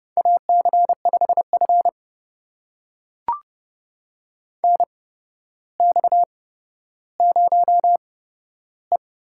Morse code, heard as a keyed tone, about 30 wpm, spaced as for 13 wpm, 700 Hz. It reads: AC5F NX0E
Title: USA Call Signs - 4 Characters - 8x Character Spacing - 30wpm